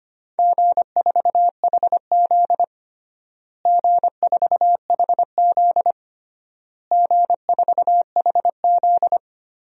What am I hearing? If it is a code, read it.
Morse code, heard as a keyed tone, about 25 wpm, 700 Hz. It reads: G4HZ G4HZ G4HZ
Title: DXCC-100 Call Signs - 4 Characters - Repeated 3x - 25wpm